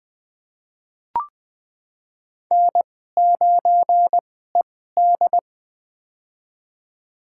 Morse code, heard as a keyed tone, about 20 wpm, spaced as for 14 wpm, 700 Hz. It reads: N9ED
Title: USA Call Signs - 2x Character Spacing - 20wpm